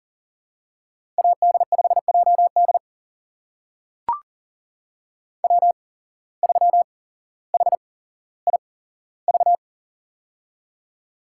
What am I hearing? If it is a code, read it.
Morse code, heard as a keyed tone, about 38 wpm, spaced as for 17 wpm, 700 Hz. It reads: AD5JD W3HIV
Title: USA Call Signs - 5 Characters - 8x Character Spacing - 40wpm